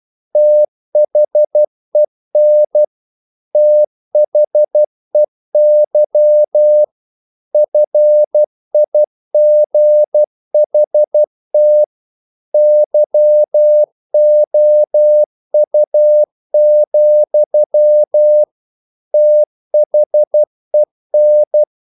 Morse code, heard as a keyed tone, about 12 wpm, 600 Hz. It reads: THEN THEY FIGHT YOU, THEN
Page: English quotations in CW (telegraphy - morse)